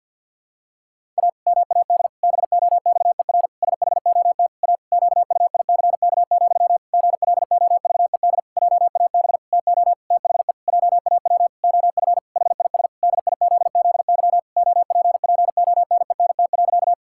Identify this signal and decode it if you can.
Morse code, heard as a keyed tone, about 52 wpm, 700 Hz. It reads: A MAD BOXER SHOT A QUICK, GLOVED JAB TO THE JAW OF HIS DIZZY OPPONENT.
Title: Pangram Challenge v2 - 50wpm